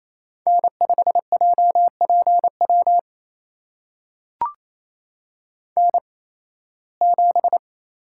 Morse code, heard as a keyed tone, about 28 wpm, spaced as for 18 wpm, 700 Hz. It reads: N5JPW N7
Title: USA Call Signs - 8x Character Spacing - 28wpm